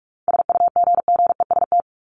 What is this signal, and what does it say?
Morse code, heard as a keyed tone, about 44 wpm, 700 Hz. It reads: SUGGEST
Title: ED Suffix - Top 100 Words with Base Words 45wpm